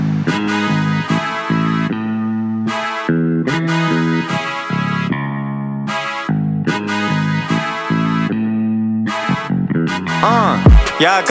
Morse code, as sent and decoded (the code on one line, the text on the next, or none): none